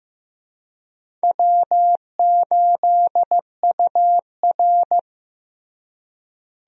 Morse code, heard as a keyed tone, about 15 wpm, 700 Hz. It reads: W8UR
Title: USA Call Signs - 4 Characters - Repeated 2x - 15wpm